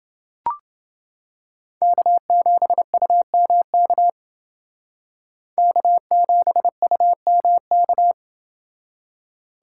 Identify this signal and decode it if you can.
Morse code, heard as a keyed tone, about 28 wpm, 700 Hz. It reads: K7UMK K7UMK